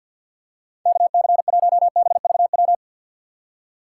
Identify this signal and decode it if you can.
Morse code, heard as a keyed tone, about 47 wpm, 700 Hz. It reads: KC1BVW
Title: USA Call Signs - 6 Characters - 3x Character Spacing - 50wpm